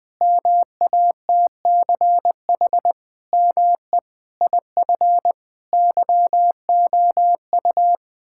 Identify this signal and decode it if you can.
Morse code, heard as a keyed tone, about 20 wpm, 700 Hz. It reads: MATCH ME IF YOU
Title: Sentences from Top 1600 Words 20wpm